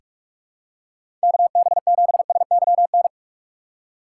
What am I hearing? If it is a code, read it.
Morse code, heard as a keyed tone, about 47 wpm, 700 Hz. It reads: KB7SYN
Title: USA Call Signs - 6 Characters - 8x Character Spacing - 45wpm